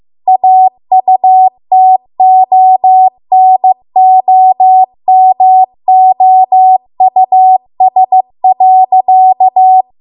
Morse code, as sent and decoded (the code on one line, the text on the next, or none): .- ..- - --- -. --- -- --- ..- ... .-.-.-
AUTONOMOUS.